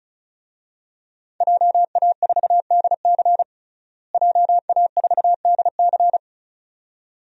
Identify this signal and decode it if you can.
Morse code, heard as a keyed tone, about 36 wpm, 700 Hz. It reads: JA4DC JA4DC
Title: DXCC-100 Call Signs - 5 Characters - Repeated 2x - 35wpm